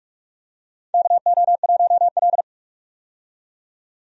Morse code, heard as a keyed tone, about 46 wpm, 700 Hz. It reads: KO1L
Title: USA Call Signs - 4 Characters - 45wpm